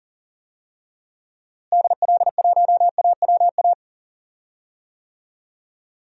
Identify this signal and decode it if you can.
Morse code, heard as a keyed tone, about 40 wpm, 700 Hz. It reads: DL1AWA
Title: DXCC-100 Call Signs - 6 Characters - Repeated 3x - 40wpm